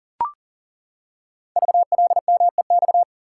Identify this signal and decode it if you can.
Morse code, heard as a keyed tone, about 37 wpm, 690 Hz. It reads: VLMEX
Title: Code Groups - 5 Letters - 40wpm